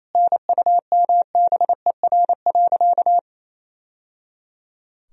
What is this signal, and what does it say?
Morse code, heard as a keyed tone, about 28 wpm, 700 Hz. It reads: NUMBER.
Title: Sentences from Top 1000 Words - Review Part 1 - 28wpm